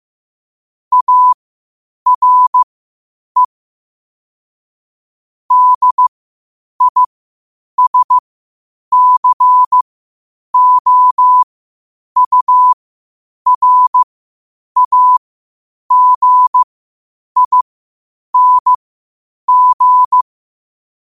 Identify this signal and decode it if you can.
Morse code, heard as a keyed tone, about 15 wpm, spaced as for 8 wpm, 1000 Hz. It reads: ARE DISCOURAGING
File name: qotdfiles / 8wpm